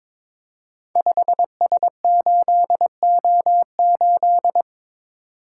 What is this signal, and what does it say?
Morse code, heard as a keyed tone, about 21 wpm, 700 Hz. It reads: 5S8O8
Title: Code Groups - 5 Letter:Numbers - 22wpm